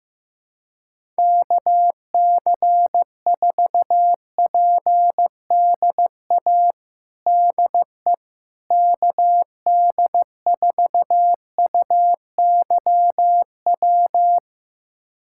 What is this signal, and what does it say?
Morse code, heard as a keyed tone, about 15 wpm, 700 Hz. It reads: KC4PDA DE KD4UYW
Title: USA Callsign DE Callsign - 6 Characters - 15wpm